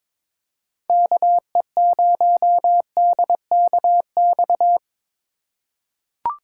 Morse code, heard as a keyed tone, about 23 wpm, 700 Hz. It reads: KE0DKX
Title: USA Call Signs - 6 Characters - 2x Character Spacing - 22wpm